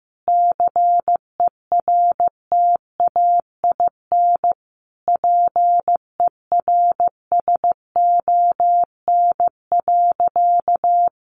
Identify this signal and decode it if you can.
Morse code, heard as a keyed tone, about 15 wpm, 700 Hz. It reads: CERTAIN PERSON.
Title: Sentences from Top 500 Words 15wpm